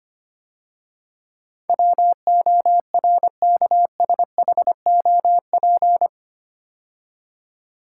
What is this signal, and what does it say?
Morse code, heard as a keyed tone, about 25 wpm, 700 Hz. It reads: WORKSHOP
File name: 8 Letter Words - First 3 Letters Given - 25wpm